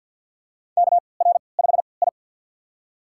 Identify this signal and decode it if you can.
Morse code, heard as a keyed tone, about 52 wpm, spaced as for 29 wpm, 700 Hz. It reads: KR5I
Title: USA Call Signs - 4 Characters - 3x Character Spacing - 50wpm